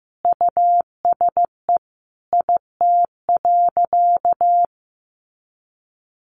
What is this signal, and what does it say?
Morse code, heard as a keyed tone, about 15 wpm, 700 Hz. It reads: USE IT.
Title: Sentences from Top 500 Words 15wpm